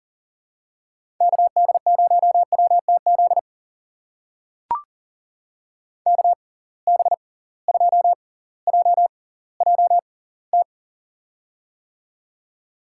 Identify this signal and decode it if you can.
Morse code, heard as a keyed tone, about 40 wpm, spaced as for 20 wpm, 700 Hz. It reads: KD0WTZ KB2JJT
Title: USA Call Signs - 6 Characters - 6x Character Spacing - 40wpm